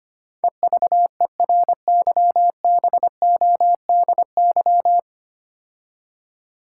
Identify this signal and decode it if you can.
Morse code, heard as a keyed tone, about 25 wpm, 700 Hz. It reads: EVERYBODY
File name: 9 Letter Words - First 3 Letters Given - 25wpm